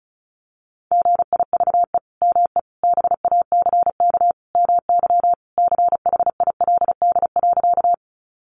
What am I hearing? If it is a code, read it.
Morse code, heard as a keyed tone, about 34 wpm, 700 Hz. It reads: GIVE ME BACK MY CHILD.
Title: Sentences from Top 800 Words 35wpm